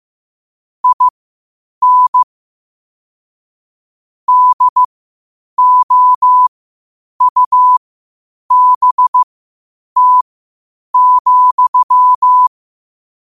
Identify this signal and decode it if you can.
Morse code, heard as a keyed tone, about 15 wpm, spaced as for 8 wpm, 1000 Hz. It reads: IN DOUBT,
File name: qotdfiles / 8wpm